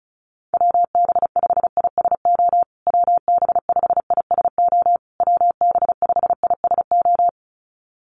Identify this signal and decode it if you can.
Morse code, heard as a keyed tone, about 34 wpm, 700 Hz. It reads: WB5ISO WB5ISO WB5ISO
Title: USA Call Signs Repeated 3x 35wpm